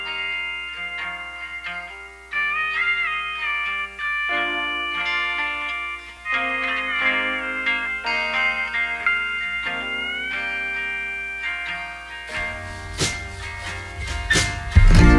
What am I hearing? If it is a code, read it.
Dial-up modem handshake.